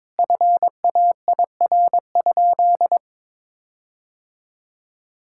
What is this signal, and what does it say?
Morse code, heard as a keyed tone, about 22 wpm, 700 Hz. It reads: FAIR?
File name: practice / 22wpm / Sentences with 4-Word Phrases - 22wpm